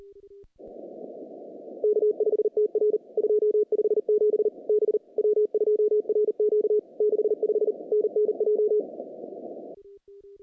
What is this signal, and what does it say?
Morse code, heard as a keyed tone, about 34 wpm, 400 Hz. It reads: K5TR 257 B W2RQ 65 NNJ
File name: contests / CW-SS-k5tr- -0000